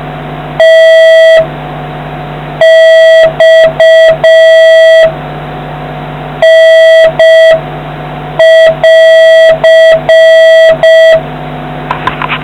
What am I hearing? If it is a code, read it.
Morse code, heard as a keyed tone, about 6 wpm, 650 Hz, with noise. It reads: T X N<AR>